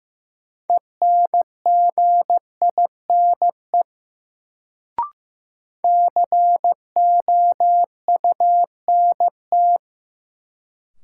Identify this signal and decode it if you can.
Morse code, heard as a keyed tone, about 15 wpm, 700 Hz. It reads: ENGINE COUNT
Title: Sentences from Top 1500 Words - New Word Drill - 15wpm